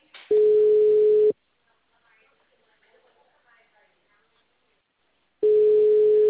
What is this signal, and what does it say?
Ringback tone (Europe, 425 Hz), with speech.
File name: out- -2004- - - .7491